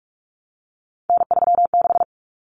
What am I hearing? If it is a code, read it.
Morse code, heard as a keyed tone, about 44 wpm, 700 Hz. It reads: N36
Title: Common Cut Numbers - 3-Digit 45wpm